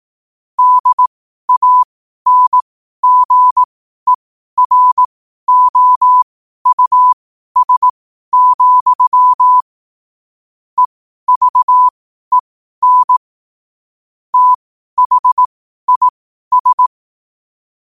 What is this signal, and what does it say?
Morse code, heard as a keyed tone, about 18 wpm, spaced as for 12 wpm, 1000 Hz. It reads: DANGEROUS, EVEN THIS
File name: 12wpm